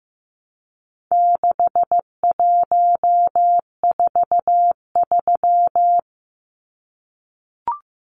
Morse code, heard as a keyed tone, about 15 wpm, 700 Hz. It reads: 6143
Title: Numbers - 4-Digit 15wpm